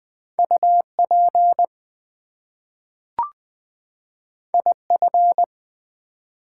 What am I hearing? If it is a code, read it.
Morse code, heard as a keyed tone, about 20 wpm, 700 Hz. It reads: UP IF
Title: Two Letter Words - 20wpm